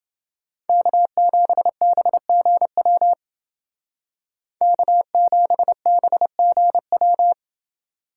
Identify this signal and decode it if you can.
Morse code, heard as a keyed tone, about 28 wpm, 700 Hz. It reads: K7BGW K7BGW